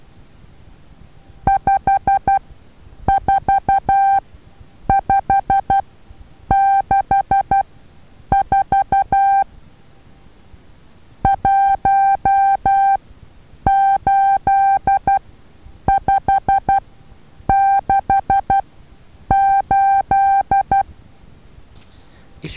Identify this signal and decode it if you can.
Morse code, heard as a keyed tone, about 12 wpm, spaced as for 8 wpm, 790 Hz. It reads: 54564 18568
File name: Taviro HA6VA A